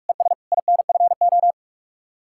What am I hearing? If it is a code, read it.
Morse code, heard as a keyed tone, about 67 wpm, spaced as for 55 wpm, 700 Hz. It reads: ES INFO
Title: Speed-Racing Common QSO Phrases - 45wpm